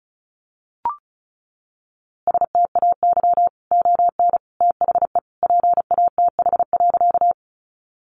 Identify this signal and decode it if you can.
Morse code, heard as a keyed tone, about 34 wpm, 700 Hz. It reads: STAY ON THE PATH.
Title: A Sound Of Thunder 35wpm